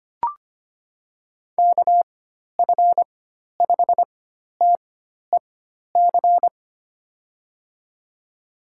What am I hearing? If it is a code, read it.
Morse code, heard as a keyed tone, about 25 wpm, spaced as for 12 wpm, 700 Hz. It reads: KF5TEC
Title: USA Call Signs - 6 Characters - 4x Character Spacing - 25wpm